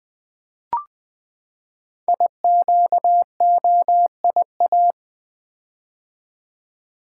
Morse code, heard as a keyed tone, about 20 wpm, 700 Hz. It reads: IQOIA
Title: Code Groups - 5 Letters - 20wpm